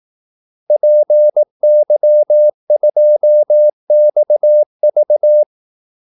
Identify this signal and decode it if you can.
Morse code, heard as a keyed tone, about 18 wpm, 600 Hz. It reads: PY2XV
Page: Kaldesignaler 18 wpm